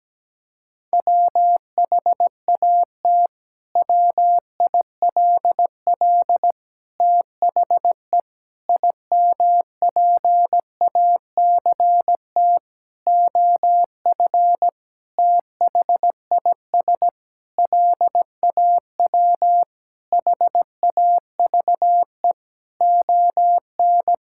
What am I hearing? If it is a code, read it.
Morse code, heard as a keyed tone, about 17 wpm, 700 Hz. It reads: WHAT WILL THE IMPACT OF THIS LAW HAVE ON
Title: Sentences from Top 1000 Words - Review Part 6 - 17wpm